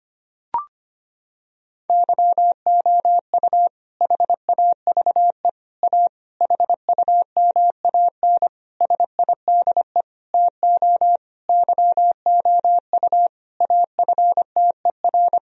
Morse code, heard as a keyed tone, about 25 wpm, 700 Hz. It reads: YOU HAVE A HUMAN SIDE TO YOU AFTER
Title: Sentences from Top 1000 Words - Review Part 3 - 25wpm